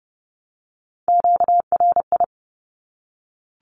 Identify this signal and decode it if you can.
Morse code, heard as a keyed tone, about 29 wpm, 700 Hz. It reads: QRI
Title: Q-Codes 30wpm